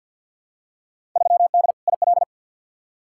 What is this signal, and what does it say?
Morse code, heard as a keyed tone, about 46 wpm, 700 Hz. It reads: 3D IL
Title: Field Day Exchanges Encore - 50wpm